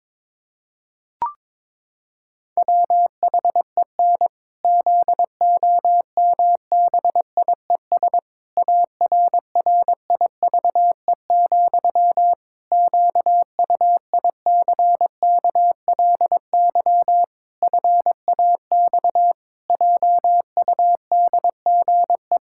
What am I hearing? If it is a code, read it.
Morse code, heard as a keyed tone, about 22 wpm, 700 Hz. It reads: WHEN ZOMBIES ARRIVE, QUICKLY FAX JUDGE
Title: Pangram Challenge v2 - 22wpm